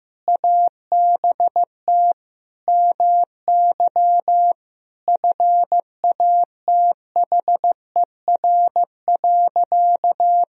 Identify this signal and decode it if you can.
Morse code, heard as a keyed tone, about 15 wpm, 700 Hz. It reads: ABT MY FATHER.
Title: Sentences from Top 400 Words and Abbreviations 15wpm